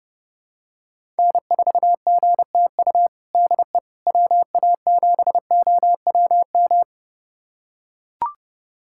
Morse code, heard as a keyed tone, about 30 wpm, 700 Hz. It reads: N4GTU DE WA7OWM
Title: USA Callsign DE Callsign 30wpm